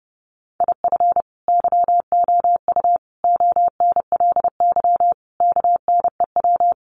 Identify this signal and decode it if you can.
Morse code, heard as a keyed tone, about 30 wpm, 700 Hz. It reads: IF YOU ONLY KNEW
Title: Sentences from Top 1500 Words 30wpm